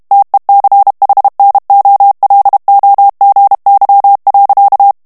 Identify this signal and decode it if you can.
Morse code, heard as a keyed tone, about 32 wpm, 800 Hz. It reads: TECHNOLOGY.
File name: morse mp3 / ANGLAIS texte clair 3 vitesse 30 mots minute